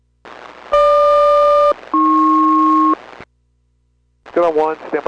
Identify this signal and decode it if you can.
Aircraft SELCAL, heard as two pairs of tones, with speech. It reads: GQ-AP